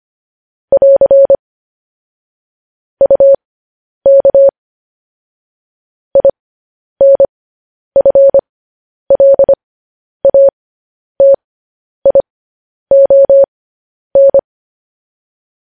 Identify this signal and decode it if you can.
Morse code, heard as a keyed tone, about 25 wpm, spaced as for 10 wpm, 550 Hz. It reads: <AR> UK INFLATION
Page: news headlines in morse code at 10 wpm